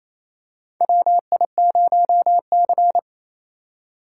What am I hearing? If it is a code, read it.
Morse code, heard as a keyed tone, about 28 wpm, 700 Hz. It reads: WI0C
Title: USA Call Signs - 4 Characters - 4x Character Spacing - 28wpm